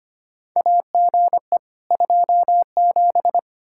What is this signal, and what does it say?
Morse code, heard as a keyed tone, about 25 wpm, 700 Hz. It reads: AGE 27
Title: K1USN SST QSO Practice 25wpm